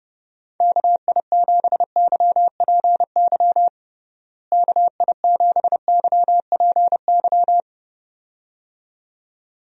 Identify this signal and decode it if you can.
Morse code, heard as a keyed tone, about 30 wpm, 700 Hz. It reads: KI7YPY KI7YPY
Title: USA Call Signs - 6 Characters - Repeated 2x - 30wpm